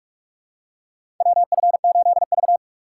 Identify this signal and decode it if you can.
Morse code, heard as a keyed tone, about 42 wpm, 700 Hz. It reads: WF8V
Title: USA Call Signs - 4 Characters - 6x Character Spacing - 45wpm